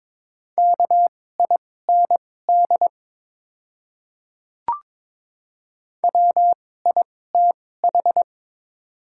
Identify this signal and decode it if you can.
Morse code, heard as a keyed tone, about 22 wpm, spaced as for 16 wpm, 700 Hz. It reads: KIND WITH
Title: 2x Character Spacing - Top 200 Words - 22wpm